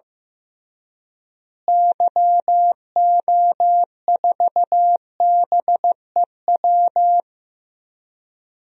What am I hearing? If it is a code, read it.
Morse code, heard as a keyed tone, about 15 wpm, 700 Hz. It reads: YO4BEW
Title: DXCC-100 Call Signs - 6 Characters - Repeated 3x - 15wpm